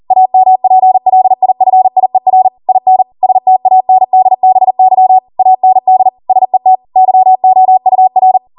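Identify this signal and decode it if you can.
Morse code, heard as a keyed tone, about 40 wpm, 750 Hz. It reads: AMPLIFIER IN STANDBY AND SET YOUR
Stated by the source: Code practice at 40 wpm.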